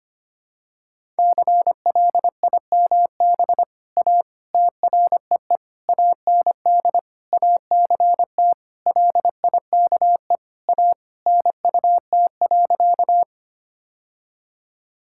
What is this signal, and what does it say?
Morse code, heard as a keyed tone, about 25 wpm, 700 Hz. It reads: CLIMB A TREE AND ACT LIKE A NUT.
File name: practice / 25wpm / All-Age Appropriate Jokes - 25wpm